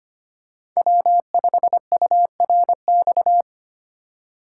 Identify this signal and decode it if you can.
Morse code, heard as a keyed tone, about 25 wpm, 700 Hz. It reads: W5URX
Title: USA Call Signs - 5 Characters - 4x Character Spacing - 25wpm